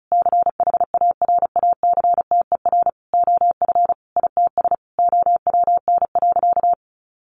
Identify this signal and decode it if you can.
Morse code, heard as a keyed tone, about 34 wpm, 700 Hz. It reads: CHARACTER OF ITS OWN.
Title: Sentences from Top 1200 Words 35wpm